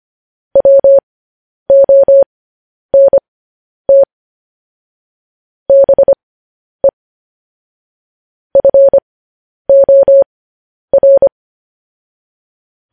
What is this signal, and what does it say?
Morse code, heard as a keyed tone, about 25 wpm, spaced as for 10 wpm, 550 Hz. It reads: WONT BE FOR
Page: news-headlines-in-morse-code-at-10-wpm